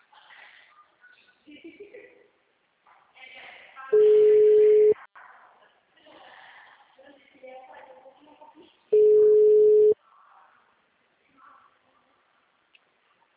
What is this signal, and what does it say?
Ringback tone (Europe, 425 Hz), with speech.